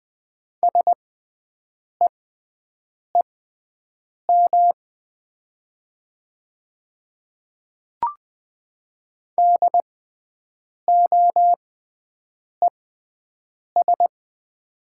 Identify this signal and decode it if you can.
Morse code, heard as a keyed tone, about 20 wpm, spaced as for 7 wpm, 700 Hz. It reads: SEEM DOES
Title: 6x Character Spacing - Top 300 Words - 20wpm